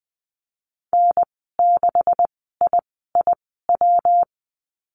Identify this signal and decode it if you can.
Morse code, heard as a keyed tone, about 20 wpm, spaced as for 15 wpm, 700 Hz. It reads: N6IIW
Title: 128 Lesson 21g - Number 6 - Call Signs 20wpm